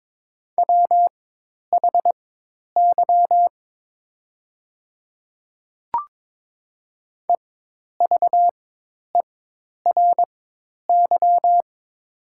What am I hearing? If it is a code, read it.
Morse code, heard as a keyed tone, about 21 wpm, spaced as for 10 wpm, 700 Hz. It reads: WHY EVERY
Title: 4x Character Spacing - Top 200 Words - 22wpm